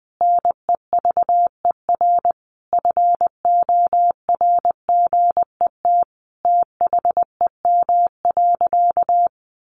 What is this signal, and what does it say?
Morse code, heard as a keyed tone, about 19 wpm, 700 Hz. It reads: NEVER FORGET THEM.
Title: Sentences from Top 1200 Words 20wpm